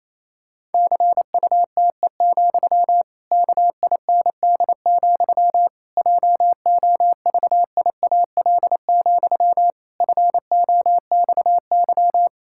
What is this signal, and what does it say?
Morse code, heard as a keyed tone, about 28 wpm, 700 Hz. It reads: CUTE, KIND, JOVIAL, FOXY
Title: Pangram Challenge v2 - 28wpm